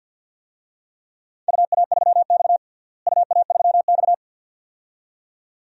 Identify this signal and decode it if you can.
Morse code, heard as a keyed tone, about 50 wpm, 700 Hz. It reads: UA3X UA3X
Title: DXCC-100 Call Signs - 4 Characters - Repeated 2x - 50wpm